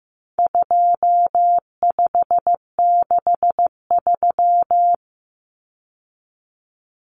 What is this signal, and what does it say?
Morse code, heard as a keyed tone, about 15 wpm, 700 Hz. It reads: 2563
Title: Common Cut Numbers - 4-Digit 15wpm